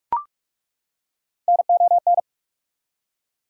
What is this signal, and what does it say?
Morse code, heard as a keyed tone, about 41 wpm, 690 Hz. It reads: NON
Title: Prefixes and Suffixes 45wpm